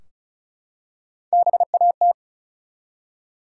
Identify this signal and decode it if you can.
Morse code, heard as a keyed tone, about 36 wpm, 700 Hz. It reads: BAT
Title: Three Letter Words - 35wpm